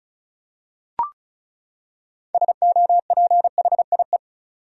Morse code, heard as a keyed tone, about 36 wpm, 700 Hz. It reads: SOPHIE
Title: USA Common Names - 35wpm